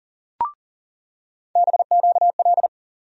Morse code, heard as a keyed tone, about 40 wpm, 700 Hz. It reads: BQL
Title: USA Call Sign Postfixes - 3 Letters 40wpm